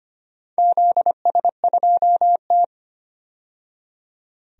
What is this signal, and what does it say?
Morse code, heard as a keyed tone, about 25 wpm, 700 Hz. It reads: ZS2T
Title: DXCC-100 Call Signs - 4 Characters - 25wpm